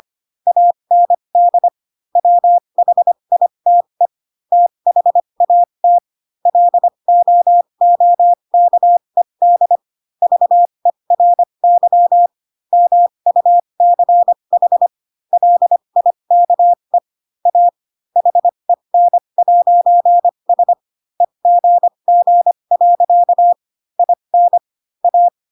Morse code, heard as a keyed tone, about 25 wpm, spaced as for 22 wpm, 700 Hz. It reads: AND WHITE THAT LOOKED VERY MUCH LIKE A HEN'S EGG. IN A